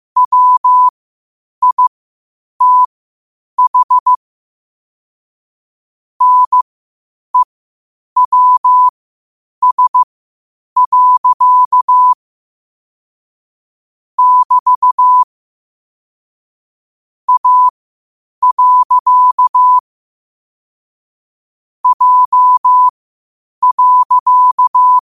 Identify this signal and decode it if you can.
Morse code, heard as a keyed tone, about 15 wpm, spaced as for 8 wpm, 1000 Hz. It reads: WITH NEWS. <BT> A. J.
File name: qotdfiles / 8wpm